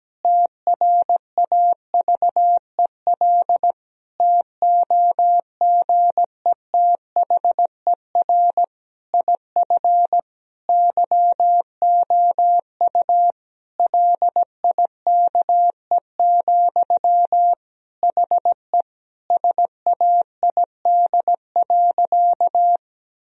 Morse code, heard as a keyed tone, about 17 wpm, 700 Hz. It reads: TRAVEL TOGETHER IF YOU LIKE, HE SAID.
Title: Sentences from Top 1000 Words - Review Part 2 - 17wpm